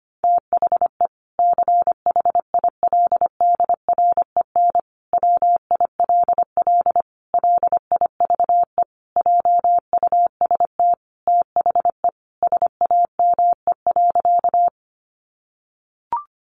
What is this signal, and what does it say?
Morse code, heard as a keyed tone, about 25 wpm, 700 Hz. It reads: THE CHILDREN WILL LIVE JUST THE SAME.
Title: Sentences from Top 300 Words 25wpm